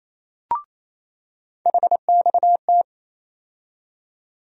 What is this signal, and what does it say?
Morse code, heard as a keyed tone, about 28 wpm, 700 Hz. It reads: HXT
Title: USA Call Sign Postfixes - 3 Letters 28wpm